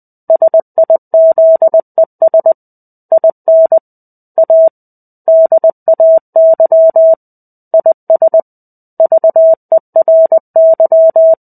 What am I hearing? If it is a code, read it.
Morse code, heard as a keyed tone, about 20 wpm, 650 Hz. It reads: SIZES IN A DAY IS VERY
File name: 05-Chapter-05 20 WPM